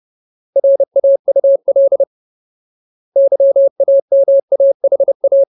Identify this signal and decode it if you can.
Morse code, heard as a keyed tone, about 30 wpm, 550 Hz. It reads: RAUL YAMAHA
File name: Ultimate Word List / 30WPM Ultimate